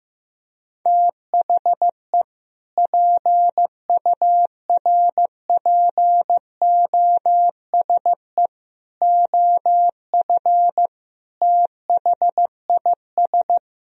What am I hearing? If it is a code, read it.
Morse code, heard as a keyed tone, about 15 wpm, 700 Hz. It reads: THE PURPOSE OF THIS
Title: Speed-Racing - Sets of 5 Words 15wpm